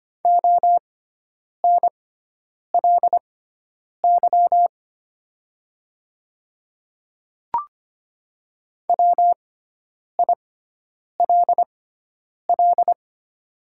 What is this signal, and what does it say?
Morse code, heard as a keyed tone, about 25 wpm, spaced as for 9 wpm, 700 Hz. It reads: ONLY WILL
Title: 6x Character Spacing - Top 200 Words - 25wpm